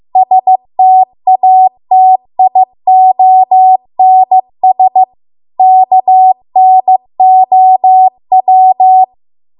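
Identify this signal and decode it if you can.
Morse code, heard as a keyed tone, about 15 wpm, 750 Hz. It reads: STATIONS KNOW
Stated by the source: Morse code practice at 15 wpm from the national amateur radio association's code archive.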